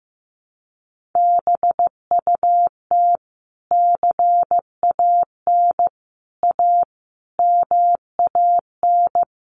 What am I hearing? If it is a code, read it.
Morse code, heard as a keyed tone, about 15 wpm, 700 Hz. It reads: BUT CAN A MAN
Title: Sentences from Top 1400 Words 15wpm